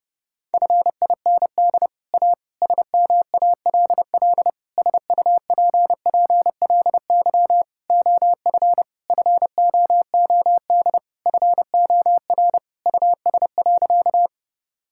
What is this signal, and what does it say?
Morse code, heard as a keyed tone, about 30 wpm, 700 Hz. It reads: FIND A SMALL SUPPLY OF FOOD FOR US.
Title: Sentences from Top 1000 Words - Review Part 5 - 30wpm